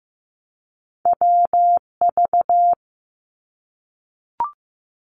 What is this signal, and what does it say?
Morse code, heard as a keyed tone, about 15 wpm, 700 Hz. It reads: WV
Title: US States Repeated 3x 15wpm